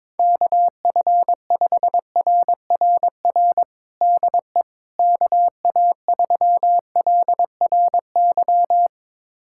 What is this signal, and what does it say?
Morse code, heard as a keyed tone, about 21 wpm, 700 Hz. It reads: KF5RRR DE KA3LRY
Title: USA Callsign DE Callsign - 6 Characters - 22wpm